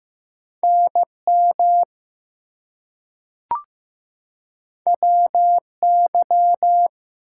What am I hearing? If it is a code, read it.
Morse code, heard as a keyed tone, about 15 wpm, 700 Hz. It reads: NM WY
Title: US States and Canadian Provinces 15wpm